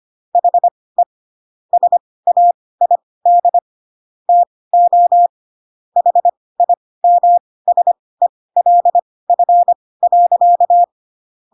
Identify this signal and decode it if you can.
Morse code, heard as a keyed tone, about 25 wpm, spaced as for 18 wpm, 700 Hz. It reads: HE SAID TO HIMSELF.
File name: Pinocchio ch3 12 25 ik1hge